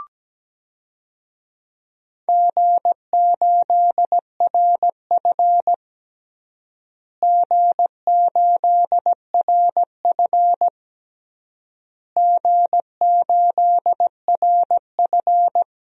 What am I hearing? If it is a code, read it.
Morse code, heard as a keyed tone, about 17 wpm, 700 Hz. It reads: G8RF G8RF G8RF
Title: DXCC-100 Call Signs - 4 Characters - Repeated 3x - 17wpm